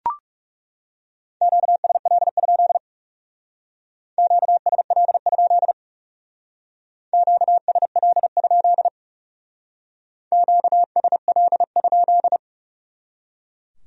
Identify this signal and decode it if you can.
Morse code, heard as a keyed tone, about 38 wpm, 700 Hz. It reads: QSL? QSL? QSL? QSL?